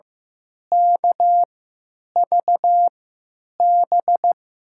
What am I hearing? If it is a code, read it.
Morse code, heard as a keyed tone, about 15 wpm, spaced as for 9 wpm, 700 Hz. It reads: KVB
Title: USA Call Signs - 3x Character Spacing - 15wpm